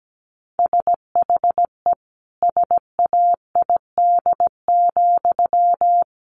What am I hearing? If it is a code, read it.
Morse code, heard as a keyed tone, about 17 wpm, 700 Hz. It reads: SHE SAID,
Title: Sentences from Top 800 Words 17wpm